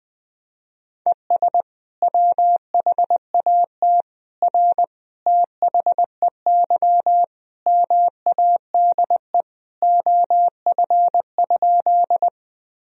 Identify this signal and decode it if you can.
Morse code, heard as a keyed tone, about 21 wpm, 700 Hz. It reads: ES WHAT R THEY MADE OF?
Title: Sentences from Top 200 Words and Abbreviations 20wpm